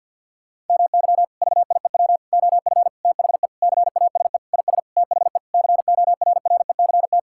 Morse code, heard as a keyed tone, about 50 wpm, 700 Hz. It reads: MY VIEW OF THE CASE IS THE CORRECT
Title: Sentences from Top 1000 Words - Review Part 6 - 50wpm